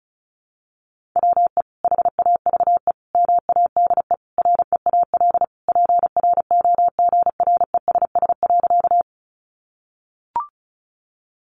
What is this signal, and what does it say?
Morse code, heard as a keyed tone, about 34 wpm, 700 Hz. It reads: WE HAVE MADE REAL PROGRESS.
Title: Sentences from Top 1400 Words 35wpm